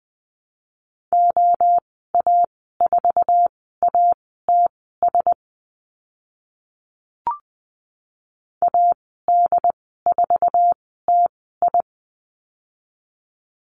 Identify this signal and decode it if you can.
Morse code, heard as a keyed tone, about 20 wpm, spaced as for 14 wpm, 700 Hz. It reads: OA4ATS AD4TI
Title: 082 Lesson 14g - Number 2 - Call Signs 20wpm